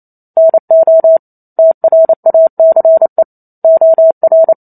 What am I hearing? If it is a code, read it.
Morse code, heard as a keyed tone, about 28 wpm, 650 Hz. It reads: NO TRACE OR
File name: Chapter 2 - 28 WPM